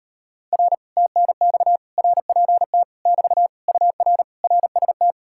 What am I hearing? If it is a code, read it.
Morse code, heard as a keyed tone, about 38 wpm, 700 Hz. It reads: R TNX RPT <BT> UR RST